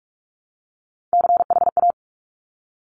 Morse code, heard as a keyed tone, about 46 wpm, 700 Hz. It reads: CHA
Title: Top Three Letter Combinations in Words 45wpm